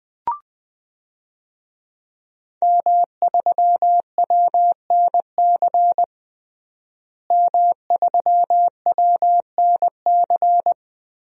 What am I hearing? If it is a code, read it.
Morse code, heard as a keyed tone, about 21 wpm, 700 Hz. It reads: M3WNC M3WNC
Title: DXCC-100 Call Signs - 5 Characters - Repeated 2x - 20wpm